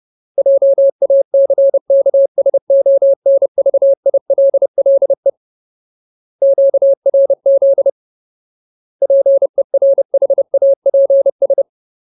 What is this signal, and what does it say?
Morse code, heard as a keyed tone, about 30 wpm, 550 Hz. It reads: JACKSONVILLE QRZ PERHAPS
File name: Ultimate Word List / 30WPM Ultimate